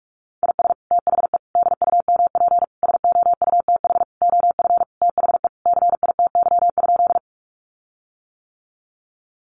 Morse code, heard as a keyed tone, about 46 wpm, 700 Hz. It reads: IS THE DUMP SOUTH OF THE CITY?
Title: Sentences from Top 1300 Words 45wpm